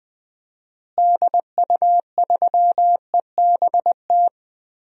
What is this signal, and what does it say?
Morse code, heard as a keyed tone, about 20 wpm, 700 Hz. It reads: DU3EBT
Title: DXCC-100 Call Signs - 6 Characters - Repeated 2x - 20wpm